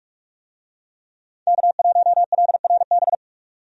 Morse code, heard as a keyed tone, about 47 wpm, 700 Hz. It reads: K1LRB
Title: USA Call Signs - 5 Characters - 45wpm